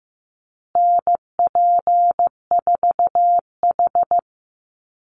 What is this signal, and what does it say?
Morse code, heard as a keyed tone, about 15 wpm, 700 Hz. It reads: NP4H
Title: Call Signs Repeated 2x - 15wpm